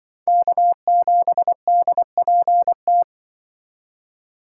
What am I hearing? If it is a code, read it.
Morse code, heard as a keyed tone, about 25 wpm, 700 Hz. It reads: K7DPT